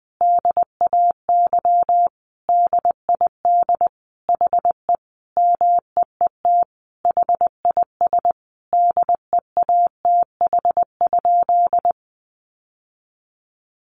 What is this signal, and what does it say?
Morse code, heard as a keyed tone, about 20 wpm, 700 Hz. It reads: DAY DID HE MEET HIS DEATH?
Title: Sentences from Top 1100 Words 20wpm